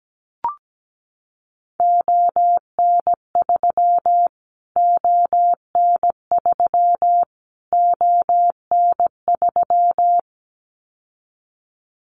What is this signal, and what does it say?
Morse code, heard as a keyed tone, about 17 wpm, 700 Hz. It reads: ON3 ON3 ON3
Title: Callsign Prefixes for Top 25 DXCC - Repeated 3x - 17wpm